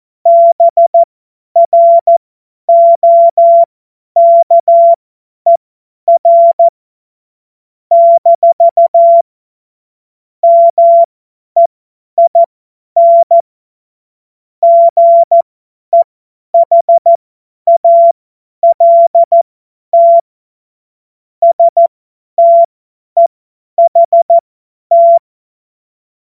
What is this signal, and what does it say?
Morse code, heard as a keyed tone, about 14 wpm, spaced as for 10 wpm, 680 Hz. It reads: BROKER - MEIN GEHALT STEHT
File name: Witz-08-Juli-2025-50-F70